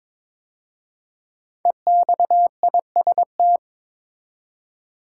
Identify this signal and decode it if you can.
Morse code, heard as a keyed tone, about 22 wpm, 700 Hz. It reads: EXIST
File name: 5 Letter Words - First 3 Letters Given - 22wpm